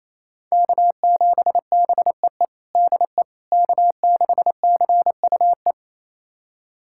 Morse code, heard as a keyed tone, about 27 wpm, 700 Hz. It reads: K7BEE DE K6CUE
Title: USA Callsign DE Callsign - 5 Characters - 28wpm